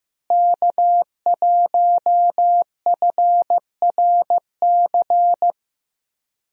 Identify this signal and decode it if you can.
Morse code, heard as a keyed tone, about 14 wpm, 700 Hz. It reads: K1FRC
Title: USA Call Signs - 5 Characters - Repeated 2x - 15wpm